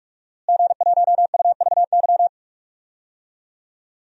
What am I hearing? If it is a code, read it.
Morse code, heard as a keyed tone, about 46 wpm, 700 Hz. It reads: G1UVY
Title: DXCC-100 Call Signs - 5 Characters - Repeated 2x - 45wpm